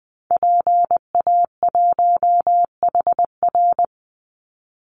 Morse code, heard as a keyed tone, about 20 wpm, 700 Hz. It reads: PA1HR
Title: Call Signs Repeated 2x - 20wpm